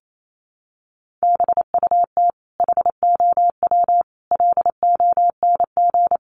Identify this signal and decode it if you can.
Morse code, heard as a keyed tone, about 28 wpm, 700 Hz. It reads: BUT HOW LONG
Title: Sentences from Top 500 Words 28wpm